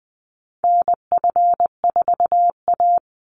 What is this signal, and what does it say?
Morse code, heard as a keyed tone, about 20 wpm, 700 Hz. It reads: NF4A
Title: Call Signs Repeated 3x - 20wpm